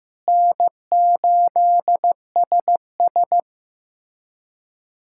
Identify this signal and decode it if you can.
Morse code, heard as a keyed tone, about 15 wpm, 700 Hz. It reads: N8SS
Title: USA Callsign DE Callsign - 4 Characters - 15wpm